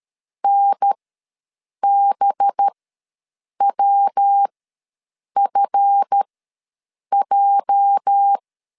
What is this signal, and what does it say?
Morse code, heard as a keyed tone, about 13 wpm, spaced as for 7 wpm, 800 Hz. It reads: NBWFJ
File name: LEZIONI / 8 YJ45 / 8 YJ ws 12wpm 3342